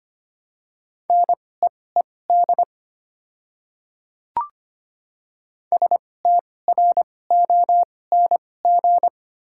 Morse code, heard as a keyed tone, about 25 wpm, spaced as for 18 wpm, 700 Hz. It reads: NEED STRONG
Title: 2x Character Spacing - Top 500 Words - 25wpm